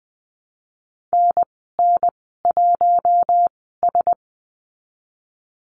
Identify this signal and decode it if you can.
Morse code, heard as a keyed tone, about 20 wpm, spaced as for 15 wpm, 700 Hz. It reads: NN1S
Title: 042 Lesson 8g - Number 1 - Call Signs 20wpm